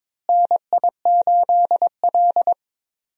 Morse code, heard as a keyed tone, about 22 wpm, 700 Hz. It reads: NI8L
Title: USA Call Signs - 4 Characters - Repeated 3x - 22wpm